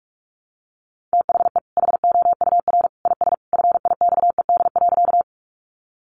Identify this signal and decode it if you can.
Morse code, heard as a keyed tone, about 44 wpm, 700 Hz. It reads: THE HOUR IS FIXED.
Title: Sentences from Top 1100 Words 45wpm